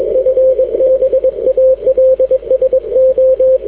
Morse code, heard as a keyed tone, about 22 wpm, 520 Hz, with radio noise. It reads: FB ALSO
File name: w5un 160m